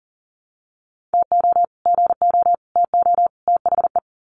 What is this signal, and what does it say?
Morse code, heard as a keyed tone, about 40 wpm, 700 Hz. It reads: TO GO TO THE
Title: Sets of 4 Words 40wpm